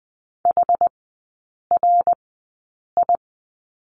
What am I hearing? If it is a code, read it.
Morse code, heard as a keyed tone, about 20 wpm, spaced as for 8 wpm, 700 Hz. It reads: HRI
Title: 060 Lesson 11e - Letter h - Three Characters 20wpm